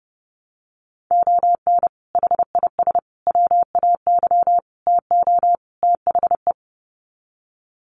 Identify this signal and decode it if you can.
Morse code, heard as a keyed tone, about 30 wpm, 700 Hz. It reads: ON HIS WAY TO THE
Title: Sets of 5 Words Encore2 30wpm